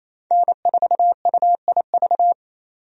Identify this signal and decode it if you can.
Morse code, heard as a keyed tone, about 28 wpm, 700 Hz. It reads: N4UIV
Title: USA Call Signs - 5 Characters - 28wpm